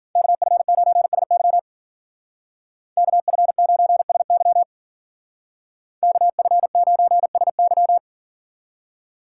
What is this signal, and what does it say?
Morse code, heard as a keyed tone, about 48 wpm, 700 Hz. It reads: KF9SY KF9SY KF9SY
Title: Speed Racing - USA 2x2 Call Signs 40wpm